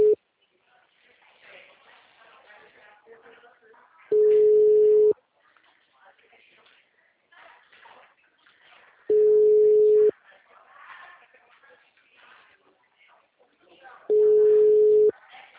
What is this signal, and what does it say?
Ringback tone (Europe, 425 Hz), with speech.
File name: out- -2020- - - .16